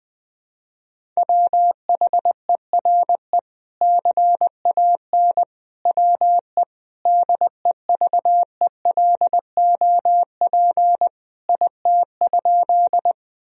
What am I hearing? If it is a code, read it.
Morse code, heard as a keyed tone, about 20 wpm, 700 Hz. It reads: WHERE CAN WE DEVELOP IT?
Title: Sentences from Top 1000 Words - Review Part 2 - 20wpm